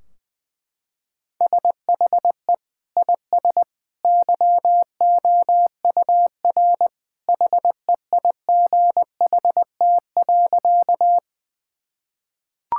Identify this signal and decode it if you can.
Morse code, heard as a keyed tone, about 20 wpm, 700 Hz. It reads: SHE IS YOUR HEIGHT.
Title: Sentences from Top 1600 Words 20wpm